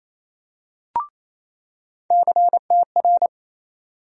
Morse code, heard as a keyed tone, about 28 wpm, 700 Hz. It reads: CTR
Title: USA Call Sign Postfixes - 3 Letters 28wpm